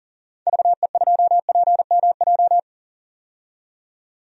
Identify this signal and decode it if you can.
Morse code, heard as a keyed tone, about 40 wpm, 700 Hz. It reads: VE2PMJ
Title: DXCC-100 Call Signs - 6 Characters - 40wpm